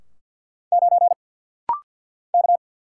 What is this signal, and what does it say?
Morse code, heard as a keyed tone, about 50 wpm, 700 Hz. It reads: 9 K
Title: Single Letter-Number - Rapid-Fire 50wpm